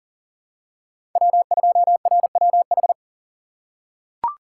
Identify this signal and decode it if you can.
Morse code, heard as a keyed tone, about 40 wpm, 690 Hz. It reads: W2RWH
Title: USA Call Signs - 5 Characters - 6x Character Spacing - 40wpm